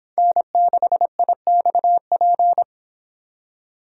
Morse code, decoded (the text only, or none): N6IXP